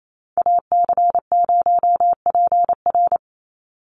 Morse code, heard as a keyed tone, about 28 wpm, 700 Hz. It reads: AC0PR
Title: NA SOTA Chaser Callsigns 28wpm